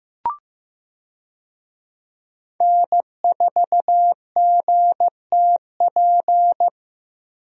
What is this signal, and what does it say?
Morse code, heard as a keyed tone, about 15 wpm, 700 Hz. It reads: N4GTP
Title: USA Call Signs - 5 Characters - Repeated 2x - 15wpm